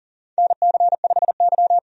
Morse code, heard as a keyed tone, about 40 wpm, 700 Hz. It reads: NC5Y
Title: USA Call Signs - 4 Characters - 40wpm